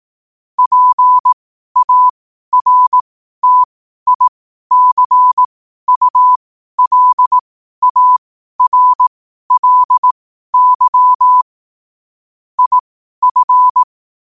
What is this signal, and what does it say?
Morse code, heard as a keyed tone, about 18 wpm, spaced as for 12 wpm, 1000 Hz. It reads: PARTICULARLY IF
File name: qotdfiles / 12wpm